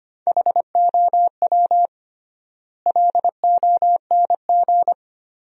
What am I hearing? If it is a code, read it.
Morse code, heard as a keyed tone, about 25 wpm, 700 Hz. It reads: HOW LONG
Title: 3x Word-Spacing - Sentences from Top 500 Words - 25wpm